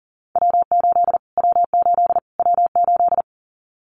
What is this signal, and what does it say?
Morse code, heard as a keyed tone, about 40 wpm, 700 Hz. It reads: W8 W8 W8
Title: Callsign Prefixes for Top 25 DXCC - Repeated 3x - 40wpm